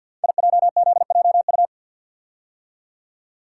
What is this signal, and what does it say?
Morse code, heard as a keyed tone, about 47 wpm, 700 Hz. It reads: IJZJU
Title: Code Groups - 5 Letter:Numbers - 50wpm